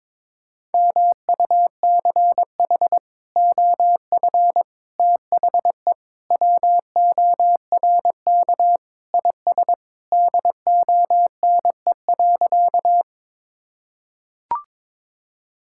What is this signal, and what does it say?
Morse code, heard as a keyed tone, about 22 wpm, 700 Hz. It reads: MUCH OF THE WORK IS DONE.
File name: practice / 22wpm / Sentences with 3-Word Phrases - 22wpm